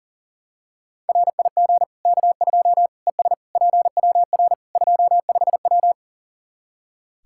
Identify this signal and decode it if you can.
Morse code, heard as a keyed tone, about 40 wpm, 700 Hz. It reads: RIG K2 ES PWR 25W
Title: Practice QSO - Exchange 2 - Info - 40wpm